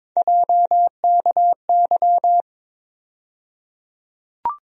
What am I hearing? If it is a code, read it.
Morse code, heard as a keyed tone, about 22 wpm, 700 Hz. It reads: JKY
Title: USA Call Sign Postfixes - 3 Letters 22wpm